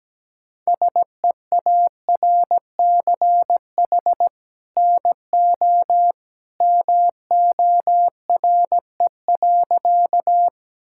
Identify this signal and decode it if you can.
Morse code, heard as a keyed tone, about 17 wpm, 700 Hz. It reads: SEARCH NO MORE.
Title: Sentences from Top 1000 Words - Review Part 5 - 17wpm